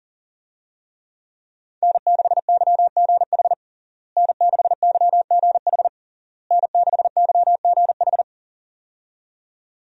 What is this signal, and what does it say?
Morse code, heard as a keyed tone, about 40 wpm, 700 Hz. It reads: N6YGH N6YGH N6YGH
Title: USA Call Signs - 5 Characters - Repeated 3x - 40wpm